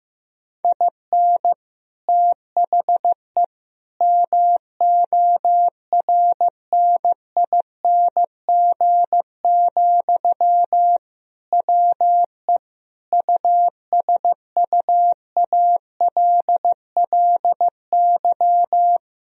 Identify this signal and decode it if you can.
Morse code, heard as a keyed tone, about 16 wpm, 700 Hz. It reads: IN THE MORNING, WE USUALLY
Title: Sentences from Top 1000 Words - Review Part 3 - 15wpm